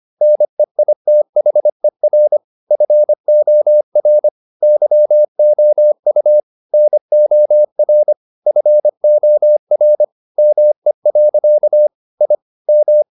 Morse code, heard as a keyed tone, about 25 wpm, 600 Hz. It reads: NEITHER FOR YOU NOR FOR ME. I M